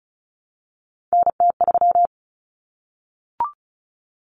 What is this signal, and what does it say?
Morse code, heard as a keyed tone, about 34 wpm, 700 Hz. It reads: NT3
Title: Common Cut Numbers - 3-Digit 35wpm